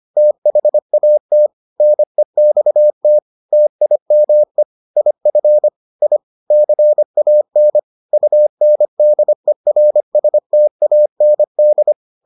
Morse code, heard as a keyed tone, about 25 wpm, 600 Hz. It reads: THAT NEXT TIME IF I CAN UNDERSTAND